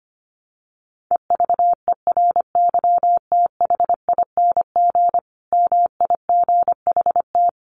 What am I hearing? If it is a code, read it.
Morse code, heard as a keyed tone, about 25 wpm, 700 Hz. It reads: EVERYTHING MIGHT
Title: Sentences from Top 1300 Words 25wpm